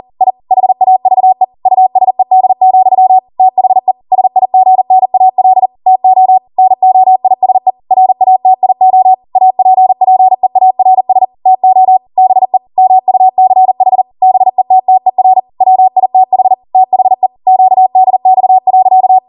Morse code, heard as a keyed tone, about 40 wpm, 750 Hz. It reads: I HAVE USED, THE SIGNAL TO NOISE RATIO APPEARS TO BE MUCH BETTER WITH THE QDX.